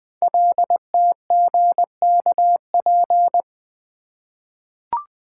Morse code, heard as a keyed tone, about 20 wpm, 700 Hz. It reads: LTGKP
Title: Code Groups - 5 Letter:Numbers - 20wpm